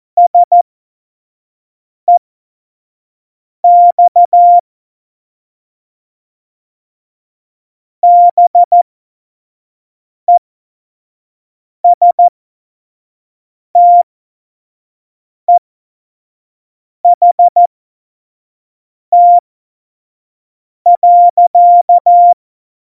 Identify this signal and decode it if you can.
Morse code, heard as a keyed tone, about 14 wpm, spaced as for 5 wpm, 700 Hz. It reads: SEX BESTEHT.
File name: Morsekurs / Witze / Witz 27. Oktober 2020, 25 F70